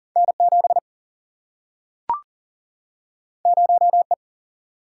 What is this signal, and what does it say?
Morse code, heard as a keyed tone, about 40 wpm, 700 Hz. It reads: N7 0E
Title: Single-Letter Single-Number 40wpm